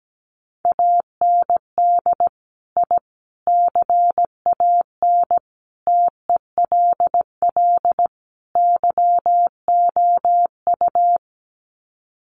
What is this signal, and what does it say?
Morse code, heard as a keyed tone, about 17 wpm, 700 Hz. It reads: AND I CAN TELL YOU
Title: Sets of 5 Words Encore2 17wpm